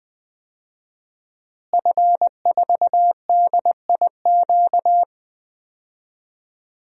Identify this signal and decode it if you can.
Morse code, heard as a keyed tone, about 20 wpm, 700 Hz. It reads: F4DIQ
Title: DXCC-100 Call Signs - 5 Characters - Repeated 3x - 20wpm